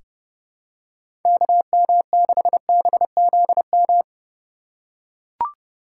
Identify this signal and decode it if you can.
Morse code, heard as a keyed tone, about 31 wpm, 700 Hz. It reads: KM6BZM
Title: USA Call Signs - 6 Characters - 4x Character Spacing - 30wpm